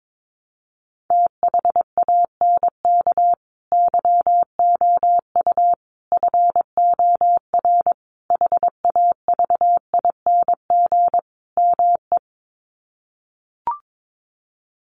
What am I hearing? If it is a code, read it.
Morse code, heard as a keyed tone, about 22 wpm, 700 Hz. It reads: THANK YOU FOR HAVING ME
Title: Sets of 5 Words Encore 22wpm